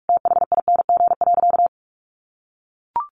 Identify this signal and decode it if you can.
Morse code, heard as a keyed tone, about 44 wpm, 700 Hz. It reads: THING.
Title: Sentences from Top 500 Words 45wpm